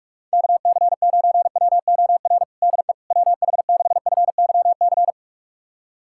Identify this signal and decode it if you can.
Morse code, heard as a keyed tone, about 46 wpm, 700 Hz. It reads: KC9WOR DE WH6FYC
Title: USA Callsign DE Callsign 45wpm